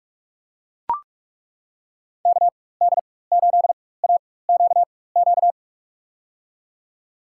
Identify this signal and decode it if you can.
Morse code, heard as a keyed tone, about 42 wpm, spaced as for 21 wpm, 700 Hz. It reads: KD8AQQ
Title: USA Call Signs - 6 Characters - 4x Character Spacing - 45wpm